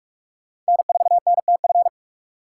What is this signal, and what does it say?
Morse code, heard as a keyed tone, about 58 wpm, 700 Hz. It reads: N4NTF